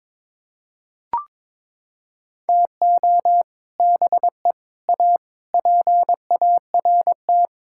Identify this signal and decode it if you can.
Morse code, heard as a keyed tone, about 22 wpm, 700 Hz. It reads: TO BE A PART
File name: practice / 22wpm / Sets of 6 Words - 22wpm